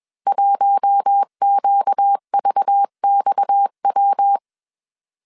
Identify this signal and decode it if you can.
Morse code, heard as a keyed tone, about 21 wpm, 800 Hz. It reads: 1QVXW